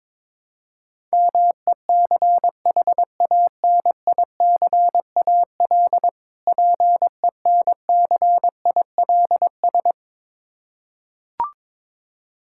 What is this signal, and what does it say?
Morse code, heard as a keyed tone, about 22 wpm, 690 Hz. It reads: MECHANICAL PENCILS
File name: Difficult Words with Hints - 22wpm